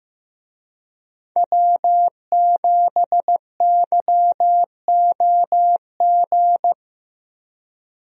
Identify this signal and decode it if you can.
Morse code, heard as a keyed tone, about 15 wpm, 700 Hz. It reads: W7YOG
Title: USA Call Signs - 5 Characters - Repeated 3x - 15wpm